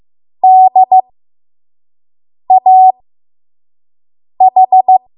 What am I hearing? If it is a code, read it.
Morse code, heard as a keyed tone, about 15 wpm, spaced as for 5 wpm, 750 Hz. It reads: DAH
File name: Morse / 05WPM